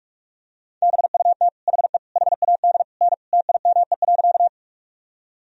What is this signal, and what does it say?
Morse code, heard as a keyed tone, about 46 wpm, 700 Hz. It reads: BUT HE HAD N TIME.
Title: Sentences from Top 100 Words and Abbreviations 45wpm